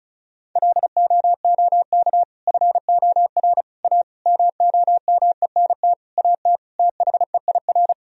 Morse code, heard as a keyed tone, about 36 wpm, 700 Hz. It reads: LOOK FOR A MOMENT AT THEIR
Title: Sentences from Top 1000 Words - Review Part 2 - 35wpm